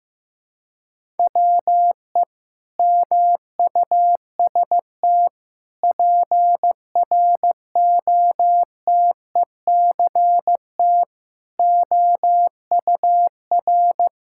Morse code, heard as a keyed tone, about 15 wpm, 700 Hz. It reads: WE MUST PROTECT OUR
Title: Sentences from Top 1000 Words - Review Part 5 - 15wpm